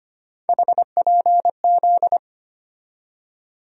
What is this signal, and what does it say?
Morse code, heard as a keyed tone, about 25 wpm, 700 Hz. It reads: HPZ
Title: USA Call Sign Postfixes - 3 Letters 25wpm